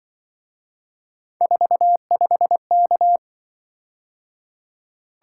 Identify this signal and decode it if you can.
Morse code, heard as a keyed tone, about 26 wpm, 700 Hz. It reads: 45K